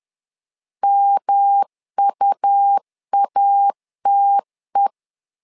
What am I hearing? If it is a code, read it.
Morse code, heard as a keyed tone, about 10 wpm, 800 Hz. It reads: MUATE
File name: LEZIONI / 3 MWRO / 3 M 10wpm 1828